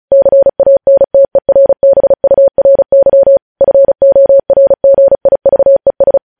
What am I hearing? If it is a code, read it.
Morse code, heard as a keyed tone, about 36 wpm, 550 Hz. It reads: CANTERBURY FORGIVES
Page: News headlines in Morse code at 35 wpm